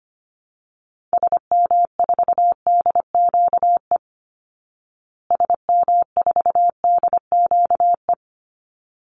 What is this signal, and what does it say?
Morse code, heard as a keyed tone, about 25 wpm, 700 Hz. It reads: SM4DQE SM4DQE
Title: Call Signs Repeated 3x - 25wpm